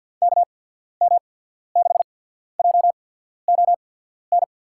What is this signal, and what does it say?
Morse code, heard as a keyed tone, about 50 wpm, spaced as for 14 wpm, 700 Hz. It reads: KM6JON
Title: USA Call Signs - 6 Characters - 8x Character Spacing - 50wpm